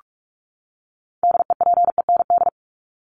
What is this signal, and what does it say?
Morse code, heard as a keyed tone, about 44 wpm, 700 Hz. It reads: DEPEND
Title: ING Suffix - Top 100 Words with Base Words 45wpm